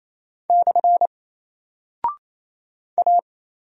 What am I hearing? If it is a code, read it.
Morse code, heard as a keyed tone, about 27 wpm, 700 Hz. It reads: / A
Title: Single Character (Letter:Number:Punctuation:Prosign) 28wpm